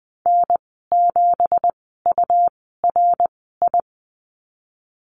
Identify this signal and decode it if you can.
Morse code, heard as a keyed tone, about 20 wpm, spaced as for 15 wpm, 700 Hz. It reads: N7URI
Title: 186 Lesson 30g - Letter b - Call Signs 20wpm